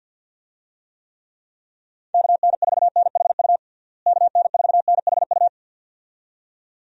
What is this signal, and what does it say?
Morse code, heard as a keyed tone, about 47 wpm, 700 Hz. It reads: KN4NHU KN4NHU
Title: USA Call Signs - 6 Characters - Repeated 2x - 50wpm